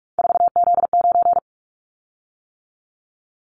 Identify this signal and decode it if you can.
Morse code, heard as a keyed tone, about 46 wpm, 700 Hz. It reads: 4Z9
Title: Callsign Prefixes for Top 50 DXCC - 45wpm